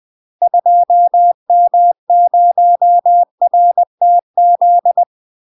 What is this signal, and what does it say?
Morse code, heard as a keyed tone, about 20 wpm, 700 Hz. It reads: 2M0RTZ
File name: UK MASTER QSOs TRAINING/20WPM